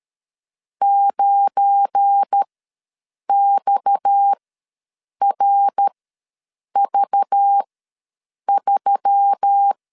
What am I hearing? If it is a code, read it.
Morse code, heard as a keyed tone, about 13 wpm, spaced as for 7 wpm, 800 Hz. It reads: 9XRV3
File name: LEZIONI / 7 GF9Z / 7 GF9 ws 12wpm 3344